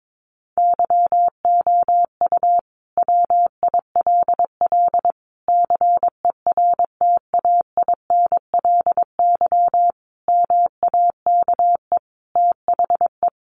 Morse code, heard as a keyed tone, about 22 wpm, 700 Hz. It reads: YOU WILL CERTAINLY MAKE THE
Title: Sentences from Top 1400 Words 22wpm